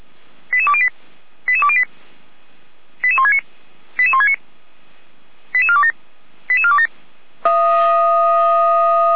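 Five-tone selective call (ZVEI), heard as quick runs of five tones, tones of about 70 ms. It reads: 88208 88208 88169 88169 88427 88427
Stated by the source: ZVEI five-tone fire brigade alerting call received by SDR, from Feuerwehr Orlamünde.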